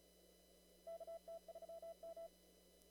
Morse code, heard as a keyed tone, about 34 wpm, 660 Hz. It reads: KT3M
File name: CW / kt5j wpx cw-op-k5pi